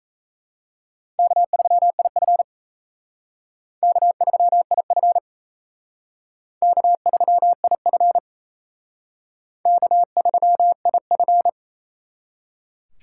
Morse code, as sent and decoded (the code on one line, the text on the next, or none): -.- ...-- .. ..-. / -.- ...-- .. ..-. / -.- ...-- .. ..-. / -.- ...-- .. ..-.
K3IF K3IF K3IF K3IF